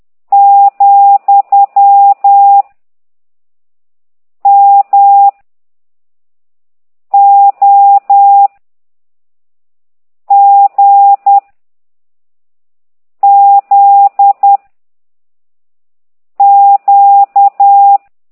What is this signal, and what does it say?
Morse code, heard as a keyed tone, about 10 wpm, 800 Hz. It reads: , M O G Z Q
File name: ct2idw / exerc13